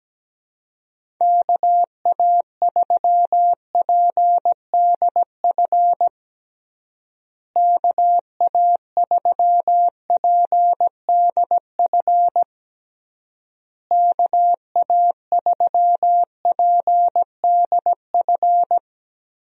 Morse code, heard as a keyed tone, about 17 wpm, 700 Hz. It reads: KA3PDF KA3PDF KA3PDF
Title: USA Call Signs - 6 Characters - Repeated 3x - 17wpm